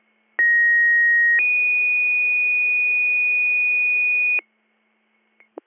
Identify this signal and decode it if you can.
Two-tone page, heard as a one-second tone, then a three-second tone, about 1900 then 2450 Hz, with noise. It is heard from 0.4 to 4.4 s.